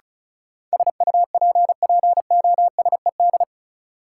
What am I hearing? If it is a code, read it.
Morse code, heard as a keyed tone, about 36 wpm, 700 Hz. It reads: SUPPOSED
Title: Sets of 3 Words - New Word Drill - 35wpm